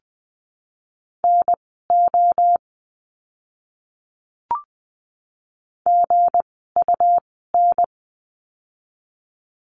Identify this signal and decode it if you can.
Morse code, heard as a keyed tone, about 20 wpm, spaced as for 15 wpm, 700 Hz. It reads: NO GUN
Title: 160 Lesson 26f - Letter g - Words 20wpm